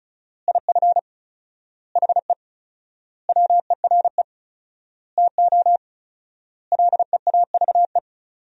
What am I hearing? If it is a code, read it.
Morse code, heard as a keyed tone, about 34 wpm, 700 Hz. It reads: IF HE WERE TO LEAVE
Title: 4x Word-Spacing - Sentences from Top 400 Words - 35wpm